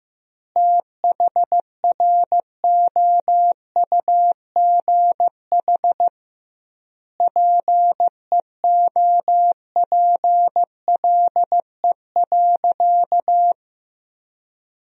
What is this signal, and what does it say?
Morse code, heard as a keyed tone, about 15 wpm, 700 Hz. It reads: THROUGH PEOPLE.
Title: 2x Word-Spacing - Sentences from Top 200 Words - 15wpm